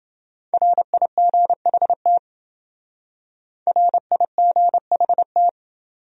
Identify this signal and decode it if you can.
Morse code, heard as a keyed tone, about 28 wpm, 700 Hz. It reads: RIGHT RIGHT